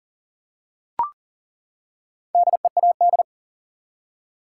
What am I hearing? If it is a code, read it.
Morse code, heard as a keyed tone, about 40 wpm, 700 Hz. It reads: DEAD
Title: Four Letter Words - 40wpm